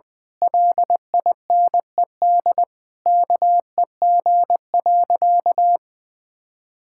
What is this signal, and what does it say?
Morse code, heard as a keyed tone, about 20 wpm, 700 Hz. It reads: LINED KEG.
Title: Pangram Challenge v2 - 20wpm